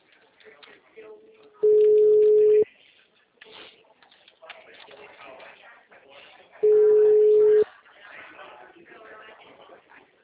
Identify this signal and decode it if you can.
Ringback tone (Europe, 425 Hz), with speech.